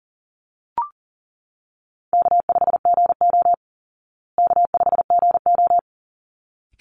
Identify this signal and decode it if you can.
Morse code, heard as a keyed tone, about 40 wpm, 700 Hz. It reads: K5GO K5GO
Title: Easy Call Signs Repeated 2x - 40wpm